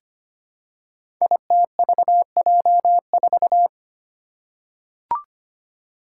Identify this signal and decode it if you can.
Morse code, heard as a keyed tone, about 25 wpm, 700 Hz. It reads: ITVJ4
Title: Five Challenging Letter-Number Combinations 25wpm